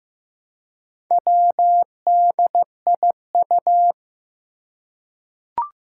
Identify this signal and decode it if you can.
Morse code, heard as a keyed tone, about 15 wpm, 700 Hz. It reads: WDIU
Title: Four Challenging Letter-Number Combinations 15wpm